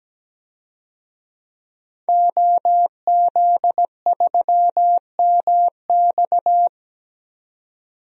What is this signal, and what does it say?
Morse code, heard as a keyed tone, about 17 wpm, 700 Hz. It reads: OZ3MX
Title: DXCC-100 Call Signs - 5 Characters - 17wpm